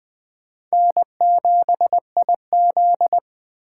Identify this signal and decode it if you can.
Morse code, heard as a keyed tone, about 20 wpm, 700 Hz. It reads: N7IZ